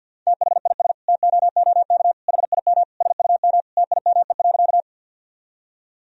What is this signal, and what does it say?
Morse code, heard as a keyed tone, about 46 wpm, 700 Hz. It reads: THIS TOOK HIM SUM TIME.
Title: Sentences from Top 300 Words and Abbreviations 50wpm